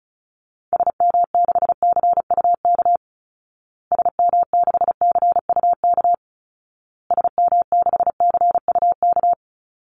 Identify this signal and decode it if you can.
Morse code, heard as a keyed tone, about 34 wpm, 700 Hz. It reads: SM6CUK SM6CUK SM6CUK
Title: Call Signs Repeated 2x - 35wpm